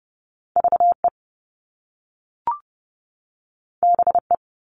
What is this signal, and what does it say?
Morse code, heard as a keyed tone, about 30 wpm, 700 Hz. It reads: VE BE
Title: Top Two Letter Combinations in Words 30wpm